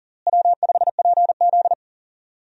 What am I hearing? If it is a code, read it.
Morse code, heard as a keyed tone, about 40 wpm, 700 Hz. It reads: W5PZ
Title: USA Call Signs - 4 Characters - Repeated 3x - 40wpm